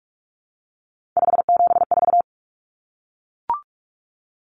Morse code, heard as a keyed tone, about 44 wpm, 700 Hz. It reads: 574
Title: Numbers - 3-Digit 45wpm